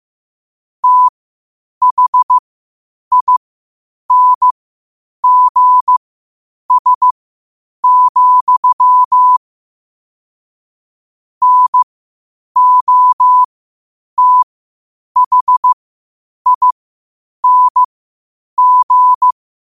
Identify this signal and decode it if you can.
Morse code, heard as a keyed tone, about 15 wpm, spaced as for 8 wpm, 1000 Hz. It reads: THINGS, NOTHING
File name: qotdfiles / 8wpm